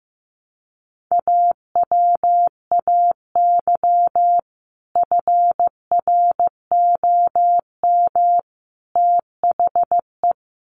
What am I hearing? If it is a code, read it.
Morse code, heard as a keyed tone, about 15 wpm, 700 Hz. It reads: AWAY FROM THE
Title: Sets of 3 Words 15wpm